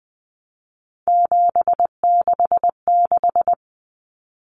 Morse code, heard as a keyed tone, about 20 wpm, 700 Hz. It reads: 766
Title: Common Cut Numbers - 3-Digit 20wpm